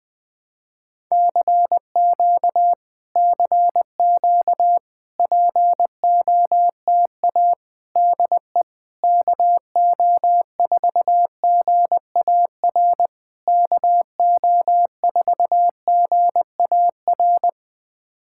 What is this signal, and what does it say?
Morse code, heard as a keyed tone, about 20 wpm, 700 Hz. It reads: CQ CQ POTA DE KO4GAR KO4GAR
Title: POTA QSO Practice - 20wpm